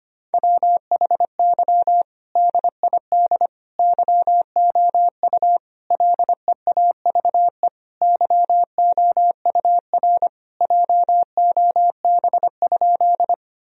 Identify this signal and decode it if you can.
Morse code, heard as a keyed tone, about 25 wpm, 700 Hz. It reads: WHY DID YOU LEAVE YOUR JOB?
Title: Sentences from Top 1000 Words - Review Part 3 - 25wpm